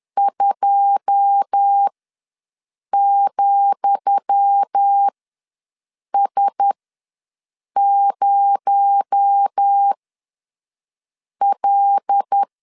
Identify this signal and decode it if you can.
Morse code, heard as a keyed tone, about 10 wpm, spaced as for 5 wpm, 800 Hz. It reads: 2,S0L
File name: LEZIONI / 10 SIGNS / 10D COMA ws 10wpm 4314